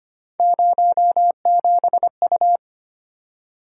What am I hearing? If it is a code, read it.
Morse code, heard as a keyed tone, about 25 wpm, 700 Hz. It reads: 07U
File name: Three Letter-Number Combinations - 25wpm